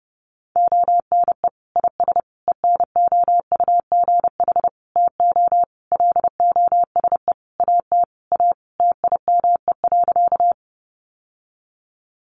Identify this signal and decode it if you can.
Morse code, heard as a keyed tone, about 30 wpm, 700 Hz. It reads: ONE IS ENOUGH TO LOSE AT A TIME.
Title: Sentences from Top 1100 Words 30wpm